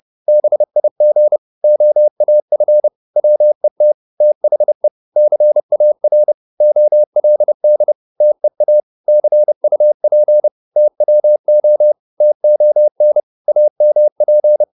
Morse code, heard as a keyed tone, about 30 wpm, 600 Hz. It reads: BIG OAF WET THE CAR OLD TEA CUP TWO TON AMP
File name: PR204 30